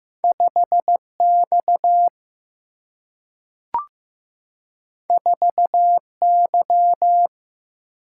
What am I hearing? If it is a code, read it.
Morse code, heard as a keyed tone, about 15 wpm, 700 Hz. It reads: 5X 4Y
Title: Single-Letter Single-Number 15wpm